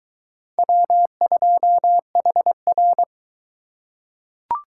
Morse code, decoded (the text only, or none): W2HR